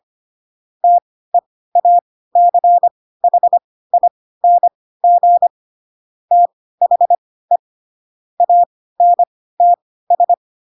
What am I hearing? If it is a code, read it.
Morse code, heard as a keyed tone, about 25 wpm, spaced as for 16 wpm, 700 Hz. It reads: TEACHING THE ANTS
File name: Pinocchio / Pinocchio ch2 10 25 ik1hge